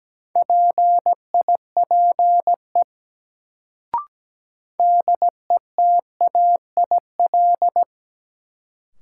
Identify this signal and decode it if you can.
Morse code, heard as a keyed tone, about 17 wpm, 700 Hz. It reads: PIPE DETAIL
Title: Sentences from Top 1500 Words - New Word Drill Alt - 17wpm